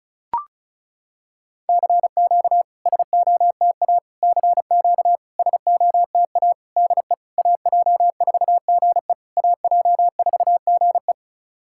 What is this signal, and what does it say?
Morse code, heard as a keyed tone, about 36 wpm, 690 Hz. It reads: CQ SOTA CQ SOTA DE AJ4GE AJ4GE
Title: SOTA QSO Practice 35wpm